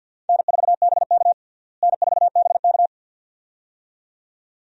Morse code, heard as a keyed tone, about 52 wpm, 700 Hz. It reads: N4BK N4BK
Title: USA Call Signs - 4 Characters - Repeated 2x - 50wpm